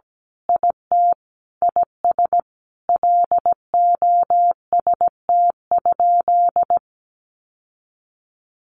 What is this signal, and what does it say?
Morse code, heard as a keyed tone, about 17 wpm, 700 Hz. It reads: IT IS LOST?
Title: Sentences from Top 1400 Words 17wpm